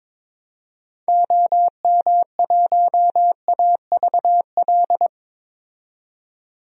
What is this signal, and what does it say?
Morse code, heard as a keyed tone, about 22 wpm, 700 Hz. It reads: OM1AVL
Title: DXCC-100 Call Signs - 6 Characters - 22wpm